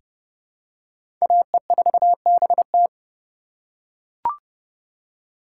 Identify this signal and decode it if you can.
Morse code, heard as a keyed tone, about 30 wpm, 700 Hz. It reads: AE4BT
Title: USA Call Signs - 5 Characters - 3x Character Spacing - 30wpm